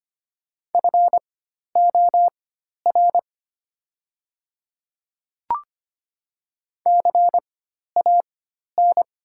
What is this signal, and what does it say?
Morse code, heard as a keyed tone, about 25 wpm, spaced as for 12 wpm, 700 Hz. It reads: FOR CAN
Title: 4x Character Spacing - Top 200 Words - 25wpm